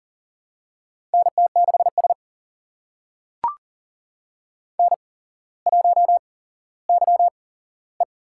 Morse code, heard as a keyed tone, about 38 wpm, spaced as for 17 wpm, 700 Hz. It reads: NT6S N1YE
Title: USA Call Signs - 4 Characters - 8x Character Spacing - 40wpm